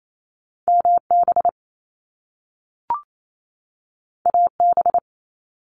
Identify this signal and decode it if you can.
Morse code, heard as a keyed tone, about 28 wpm, 700 Hz. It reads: MB AB
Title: Canadian Provinces and Territories 28wpm